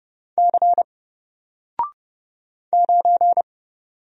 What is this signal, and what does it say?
Morse code, heard as a keyed tone, about 29 wpm, 700 Hz. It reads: C 9
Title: Single Character (Letter:Number:Punctuation:Prosign) 30wpm